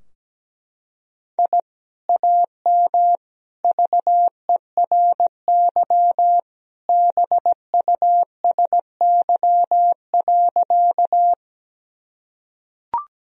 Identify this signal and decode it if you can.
Morse code, heard as a keyed tone, about 17 wpm, 700 Hz. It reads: I AM VERY BUSY.
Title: Sentences from Top 1000 Words - Review Part 3 - 17wpm